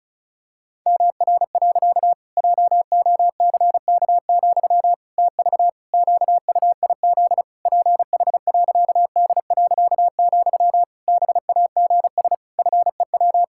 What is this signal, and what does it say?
Morse code, heard as a keyed tone, about 36 wpm, 700 Hz. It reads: MR. JOCK, TV QUIZ PH.D., BAGS FEW
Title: Pangram Challenge v2 - 35wpm